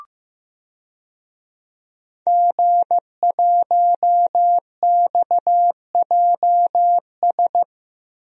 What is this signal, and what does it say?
Morse code, heard as a keyed tone, about 15 wpm, 700 Hz. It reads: G1XJS
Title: DXCC-100 Call Signs - 5 Characters - 15wpm